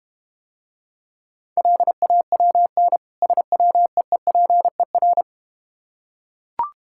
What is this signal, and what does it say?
Morse code, heard as a keyed tone, about 32 wpm, 700 Hz. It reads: LAWN SWEEPER